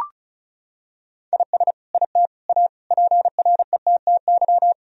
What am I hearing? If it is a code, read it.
Morse code, heard as a keyed tone, about 36 wpm, 700 Hz. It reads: IS IT A PRETTY
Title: Sentences from Top 1000 Words - Review Part 2 - 35wpm